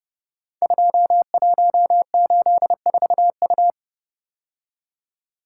Morse code, heard as a keyed tone, about 29 wpm, 700 Hz. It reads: 2184U
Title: Code Groups - 5 Letter:Numbers - 30wpm